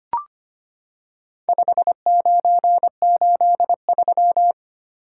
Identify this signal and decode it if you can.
Morse code, heard as a keyed tone, about 25 wpm, 700 Hz. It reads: 5983
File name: practice / 25wpm / Four Letter-Number Combinations - 25wpm